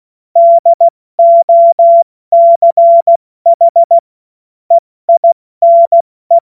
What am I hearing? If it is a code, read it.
Morse code, heard as a keyed tone, about 16 wpm, 680 Hz. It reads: DOCH EINE
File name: Morsekurs / Witze / Witz 02 April 2024 70 F80